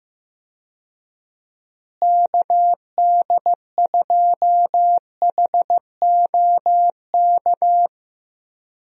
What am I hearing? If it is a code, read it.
Morse code, heard as a keyed tone, about 15 wpm, 700 Hz. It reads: KD2HOK
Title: USA Call Signs - 6 Characters - Repeated 2x - 15wpm